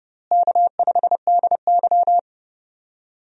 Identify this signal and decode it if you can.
Morse code, heard as a keyed tone, about 30 wpm, 700 Hz. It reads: K5DY
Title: USA Call Signs - 4 Characters - Repeated 3x - 30wpm